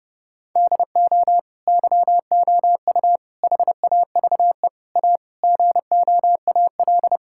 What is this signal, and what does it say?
Morse code, heard as a keyed tone, about 30 wpm, 700 Hz. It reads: DO YOU HAVE A GOAL
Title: Sentences from Top 1000 Words - Review Part 5 - 30wpm